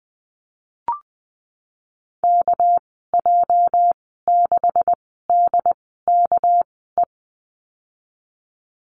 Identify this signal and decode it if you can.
Morse code, heard as a keyed tone, about 20 wpm, spaced as for 15 wpm, 700 Hz. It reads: KJ6DKE
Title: 213 Lesson 34g - Number 8 - Call Signs 20wpm